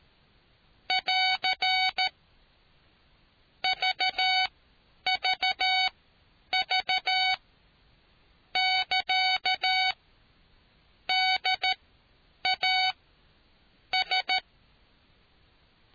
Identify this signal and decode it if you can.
Morse code, heard as a keyed tone, about 13 wpm, spaced as for 8 wpm, 750 Hz. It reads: <AR> VVV <KA> DAS